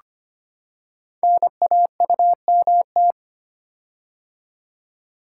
Morse code, heard as a keyed tone, about 25 wpm, 700 Hz. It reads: NAUMT
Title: Code Groups - 5 Letters - 25wpm